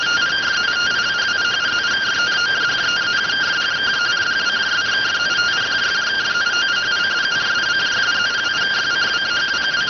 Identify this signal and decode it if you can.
Two-tone teleprinter signal, about 170 Hz shift.